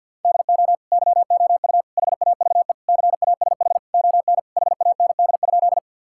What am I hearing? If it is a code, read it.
Morse code, heard as a keyed tone, about 50 wpm, 700 Hz. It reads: DO YOU HAVE CASH ON HAND?
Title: Sentences from Top 1000 Words - Review Part 5 - 50wpm